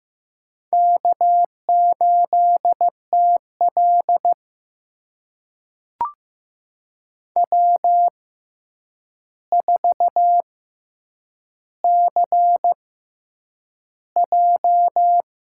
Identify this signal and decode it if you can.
Morse code, heard as a keyed tone, about 15 wpm, spaced as for 8 wpm, 700 Hz. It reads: K8TL W4CJ
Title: USA Call Signs - 4 Characters - 6x Character Spacing - 15wpm